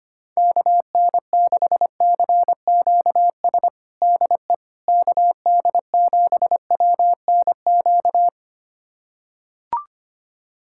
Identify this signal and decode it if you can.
Morse code, heard as a keyed tone, about 25 wpm, 700 Hz. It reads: KN6CQS DE KD7WNQ
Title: USA Callsign DE Callsign - 6 Characters - 25wpm